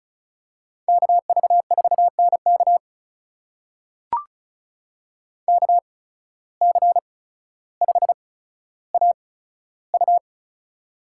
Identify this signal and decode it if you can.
Morse code, heard as a keyed tone, about 34 wpm, spaced as for 15 wpm, 700 Hz. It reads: KV4NK KC5AU
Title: USA Call Signs - 5 Characters - 8x Character Spacing - 35wpm